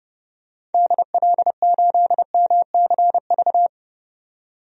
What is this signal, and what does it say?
Morse code, heard as a keyed tone, about 30 wpm, 700 Hz. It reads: DL8MCV
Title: DXCC-100 Call Signs - 6 Characters - 30wpm